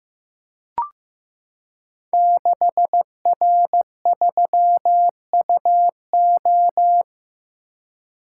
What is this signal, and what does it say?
Morse code, heard as a keyed tone, about 15 wpm, 700 Hz. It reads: 6R3UO
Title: Code Groups - 5 Letter:Numbers - 15wpm